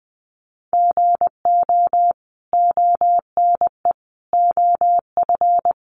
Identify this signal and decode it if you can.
Morse code, heard as a keyed tone, about 20 wpm, 700 Hz. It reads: GO ONE OF
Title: Sentences from Top 1100 Words 20wpm